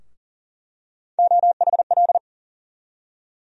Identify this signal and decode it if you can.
Morse code, heard as a keyed tone, about 40 wpm, 700 Hz. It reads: OHL
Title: USA Call Sign Postfixes - 3 Letters 40wpm